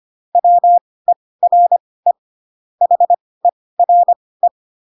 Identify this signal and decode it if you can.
Morse code, heard as a keyed tone, about 25 wpm, spaced as for 18 wpm, 700 Hz. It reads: WERE HERE